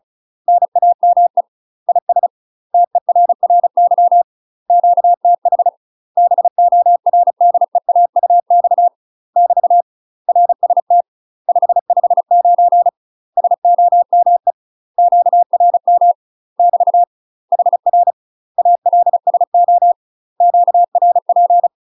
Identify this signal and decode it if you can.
Morse code, heard as a keyed tone, about 36 wpm, 700 Hz. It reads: NAME IS TERRY QTH BORDEAUX <BT> RST 559 SOME QRM <BT> HR ALSO QRP
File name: UK MASTER QSOs TRAINING / 35WPM / UK MASTER QSOs 0031